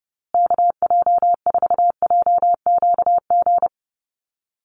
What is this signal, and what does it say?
Morse code, heard as a keyed tone, about 28 wpm, 700 Hz. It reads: KJ4JQG
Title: USA Call Signs Repeated 3x 30wpm